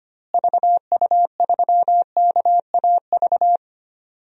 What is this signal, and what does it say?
Morse code, heard as a keyed tone, about 25 wpm, 700 Hz. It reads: VU3KAV
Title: DXCC-100 Call Signs - 6 Characters - Repeated 3x - 25wpm